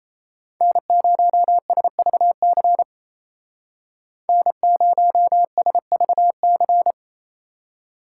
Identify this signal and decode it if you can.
Morse code, heard as a keyed tone, about 35 wpm, 700 Hz. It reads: N0SVC N0SVC
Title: Speed Racing - USA 1x3 Call Signs 28wpm